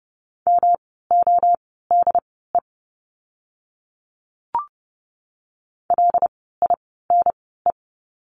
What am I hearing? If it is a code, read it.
Morse code, heard as a keyed tone, about 30 wpm, spaced as for 17 wpm, 700 Hz. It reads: MODE LINE
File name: course / 30:10wpm / 108 Lesson 18f - Letter m - Words 30wpm